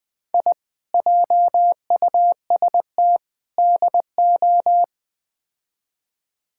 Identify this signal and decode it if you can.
Morse code, heard as a keyed tone, about 21 wpm, 700 Hz. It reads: I JUST DO
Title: Build A Phrase - 4-Word Phrases - 20wpm